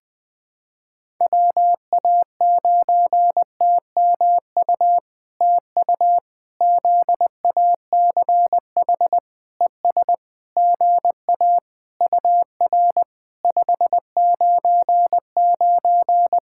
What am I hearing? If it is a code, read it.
Morse code, heard as a keyed tone, about 20 wpm, 700 Hz. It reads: WA9TMU TU ZACH ES GA UR 599
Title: POTA QSO Practice - 20wpm